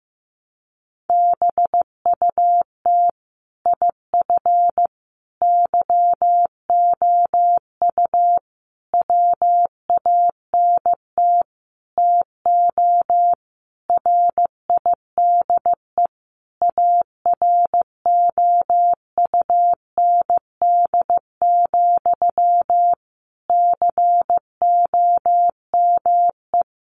Morse code, heard as a keyed tone, about 15 wpm, 700 Hz. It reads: BUT IF YOU WANT TO RIDE AROUND, COME
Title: Sentences from Top 1200 Words 15wpm